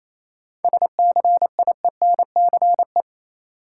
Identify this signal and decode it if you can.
Morse code, heard as a keyed tone, about 28 wpm, 700 Hz. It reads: SCIENCE
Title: Sentences from Top 1000 Words - Review Part 2 - New Word Drill - 28wpm